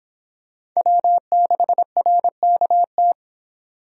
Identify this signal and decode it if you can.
Morse code, heard as a keyed tone, about 26 wpm, 700 Hz. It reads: W6RKT